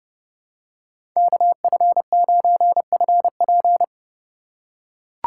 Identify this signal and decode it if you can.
Morse code, heard as a keyed tone, about 30 wpm, 700 Hz. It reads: KF9FP
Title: USA Call Signs - 5 Characters - 4x Character Spacing - 30wpm